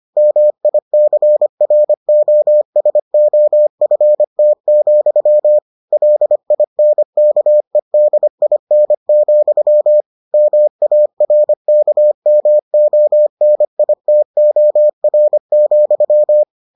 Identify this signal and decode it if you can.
Morse code, heard as a keyed tone, about 25 wpm, 600 Hz. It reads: MICROSOFT, LINKEDIN, MARKMONITOR,